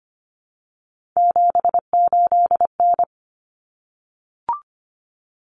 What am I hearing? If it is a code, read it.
Morse code, heard as a keyed tone, about 25 wpm, 700 Hz. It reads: 78N
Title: Common Cut Numbers - 3-Digit 25wpm